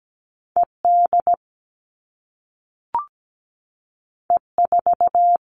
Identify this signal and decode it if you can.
Morse code, heard as a keyed tone, about 17 wpm, 700 Hz. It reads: ED E4
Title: All Cut Numbers - 2-Digit 17wpm